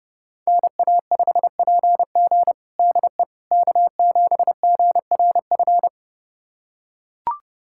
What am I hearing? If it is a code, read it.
Morse code, heard as a keyed tone, about 32 wpm, 700 Hz. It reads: NA5PG DE K7GRF
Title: USA Callsign DE Callsign - 5 Characters - 30wpm